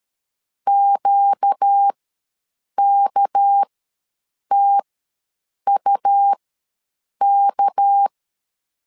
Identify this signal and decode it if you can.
Morse code, heard as a keyed tone, about 13 wpm, spaced as for 7 wpm, 800 Hz. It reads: QKTUK
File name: LEZIONI / 6 PKX0 / 6 PK ws 12wpm 3116